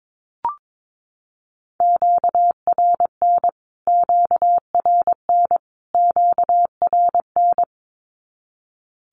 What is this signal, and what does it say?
Morse code, heard as a keyed tone, about 22 wpm, 700 Hz. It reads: QRN QRN QRN
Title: Q-Codes Repeated 3x 22wpm